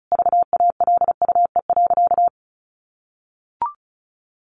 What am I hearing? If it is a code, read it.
Morse code, heard as a keyed tone, about 34 wpm, 700 Hz. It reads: VALUE.
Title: Sentences from Top 1100 Words 35wpm